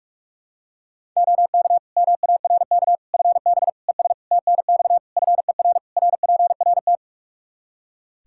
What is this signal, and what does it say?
Morse code, heard as a keyed tone, about 46 wpm, 700 Hz. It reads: OK MARK FB ES TNX FER RPRT
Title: Practice QSO - Exchange 2 - Info - 45wpm